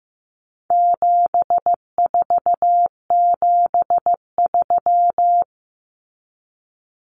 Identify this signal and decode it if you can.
Morse code, heard as a keyed tone, about 15 wpm, 700 Hz. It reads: 7473
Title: Common Cut Numbers - 4-Digit 15wpm